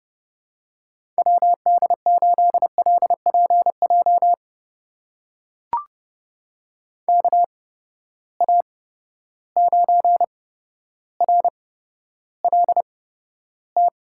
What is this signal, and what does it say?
Morse code, heard as a keyed tone, about 31 wpm, spaced as for 13 wpm, 700 Hz. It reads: WD8LPJ KA9RLT
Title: USA Call Signs - 6 Characters - 8x Character Spacing - 30wpm